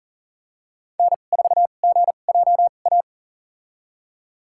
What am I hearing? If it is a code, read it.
Morse code, heard as a keyed tone, about 38 wpm, spaced as for 28 wpm, 700 Hz. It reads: N4GJA
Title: USA Call Signs - 5 Characters - 2x Character Spacing - 40wpm